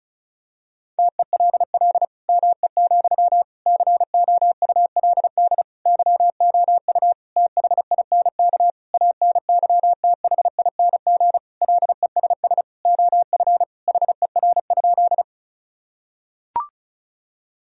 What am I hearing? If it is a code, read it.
Morse code, heard as a keyed tone, about 34 wpm, 700 Hz. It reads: TELL ME, COULD YOU THINK ANYTHING LESS OF HER?
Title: Sentences from Top 1000 Words - Review Part 2 - 35wpm